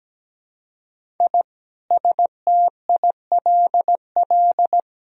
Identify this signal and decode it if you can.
Morse code, heard as a keyed tone, about 17 wpm, 700 Hz. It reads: I STILL
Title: Sentences from Top 1000 Words - Review Part 1 - 17wpm